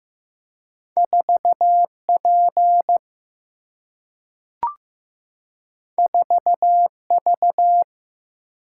Morse code, heard as a keyed tone, about 15 wpm, 700 Hz. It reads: 4P 4V
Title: Single-Letter Single-Number 15wpm